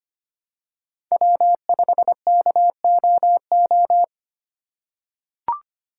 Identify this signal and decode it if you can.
Morse code, heard as a keyed tone, about 25 wpm, 700 Hz. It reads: W5KOO
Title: USA Call Signs - 5 Characters - 4x Character Spacing - 25wpm